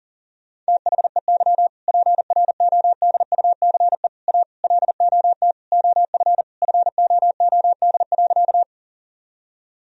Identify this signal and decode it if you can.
Morse code, heard as a keyed tone, about 40 wpm, 700 Hz. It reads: THEY PRODUCE A LOT OF FOOD.
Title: Sentences from Top 1000 Words - Review Part 3 - 40wpm